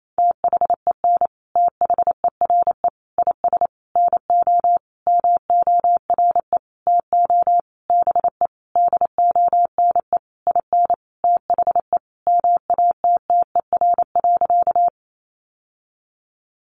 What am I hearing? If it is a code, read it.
Morse code, heard as a keyed tone, about 28 wpm, 700 Hz. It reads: THEN THERE IS NO MORE TO BE DONE IN THE MATTER.
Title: Sentences from Top 700 Words 28wpm